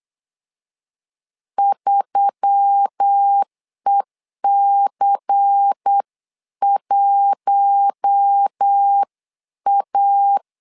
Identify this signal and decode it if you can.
Morse code, heard as a keyed tone, about 8 wpm, 800 Hz. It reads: 3EC1A
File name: LEZIONI / 5 LBVC / 5 LBVC 8wpm 2935